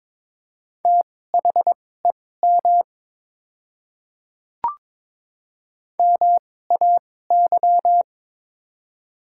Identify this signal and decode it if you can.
Morse code, heard as a keyed tone, about 21 wpm, spaced as for 16 wpm, 700 Hz. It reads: THEM MAY
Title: 2x Character Spacing - Top 100 Words - 22wpm